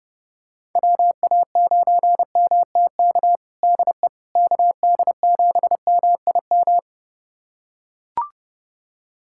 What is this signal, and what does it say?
Morse code, heard as a keyed tone, about 30 wpm, 700 Hz. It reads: WA9MTK DE KD7MIM
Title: USA Callsign DE Callsign - 6 Characters - 30wpm